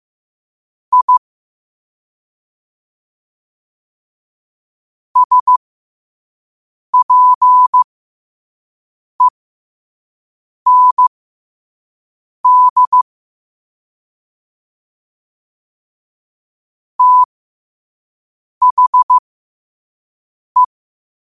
Morse code, heard as a keyed tone, about 15 wpm, spaced as for 5 wpm, 1000 Hz. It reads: I SPEND THE
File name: qotdfiles / 5wpm